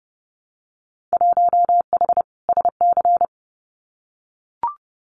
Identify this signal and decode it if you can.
Morse code, heard as a keyed tone, about 30 wpm, 700 Hz. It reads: 1H SC
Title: Winter Field Day Exchanges 30wpm